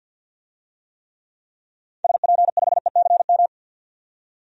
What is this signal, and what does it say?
Morse code, heard as a keyed tone, about 47 wpm, 700 Hz. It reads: SP5ECM
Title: DXCC-100 Call Signs - 6 Characters - 50wpm